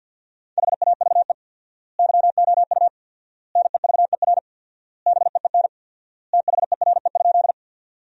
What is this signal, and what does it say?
Morse code, heard as a keyed tone, about 53 wpm, 700 Hz. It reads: HAVE YOU NEVER BEEN THERE?
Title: 4x Word-Spacing - Sentences from Top 300 Words - 50wpm